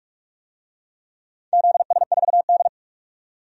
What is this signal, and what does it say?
Morse code, heard as a keyed tone, about 44 wpm, 700 Hz. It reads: ZS4D
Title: DXCC-100 Call Signs - 4 Characters - 45wpm